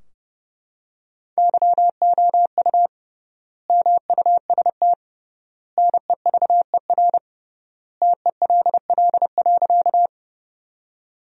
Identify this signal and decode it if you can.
Morse code, heard as a keyed tone, about 32 wpm, 700 Hz. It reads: YOU MUST NEVER TELL.
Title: 3x Word-Spacing - Sentences from Top 300 Words - 30wpm